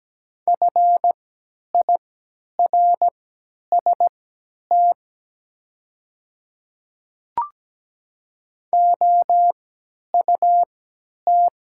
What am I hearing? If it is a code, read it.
Morse code, heard as a keyed tone, about 17 wpm, spaced as for 10 wpm, 700 Hz. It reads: FIRST OUT
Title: 3x Character Spacing - Top 200 Words - 17wpm